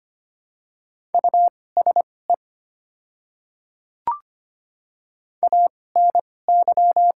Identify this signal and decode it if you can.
Morse code, heard as a keyed tone, about 25 wpm, spaced as for 18 wpm, 690 Hz. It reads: USE ANY
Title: 2x Character Spacing - Top 200 Words - 25wpm